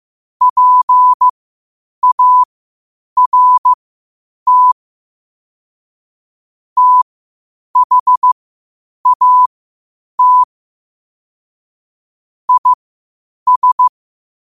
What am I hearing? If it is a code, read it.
Morse code, heard as a keyed tone, about 15 wpm, spaced as for 8 wpm, 1000 Hz. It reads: PART THAT IS
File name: qotdfiles / 8wpm